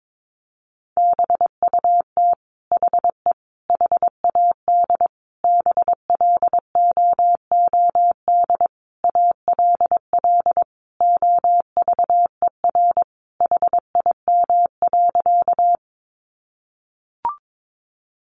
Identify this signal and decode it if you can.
Morse code, heard as a keyed tone, about 22 wpm, 700 Hz. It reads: BUT HE HAD BLOOD ALL OVER HIM.
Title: Sentences from Top 1100 Words 22wpm